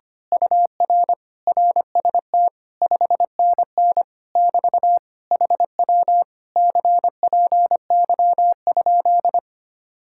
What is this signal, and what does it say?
Morse code, heard as a keyed tone, about 24 wpm, 700 Hz. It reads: UR RST 5NN <BT> HW CPY?
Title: Speed-Racing Common QSO Phrases - 25wpm